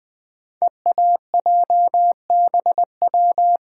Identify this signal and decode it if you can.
Morse code, heard as a keyed tone, about 20 wpm, 700 Hz. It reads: EAJBW
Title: Code Groups - 5 Letters - 20wpm